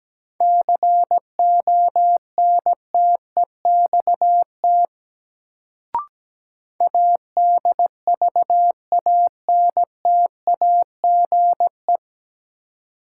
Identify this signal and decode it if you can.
Morse code, heard as a keyed tone, about 17 wpm, 700 Hz. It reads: CONTEXT ADVANTAGE
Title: Sets of 4 Words - New Word Drill - 17wpm